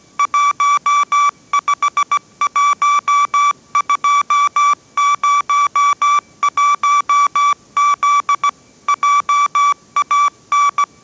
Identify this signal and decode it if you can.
Morse code, heard as a keyed tone, about 17 wpm, 1160 Hz. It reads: 151201Z JAN